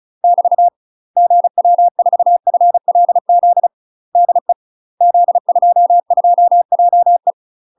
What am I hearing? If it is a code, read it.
Morse code, heard as a keyed tone, about 36 wpm, 700 Hz. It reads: <BT> GW4FLZ DE Z22JE
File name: UK MASTER QSOs TRAINING / 35WPM / UK MASTER QSOs 0020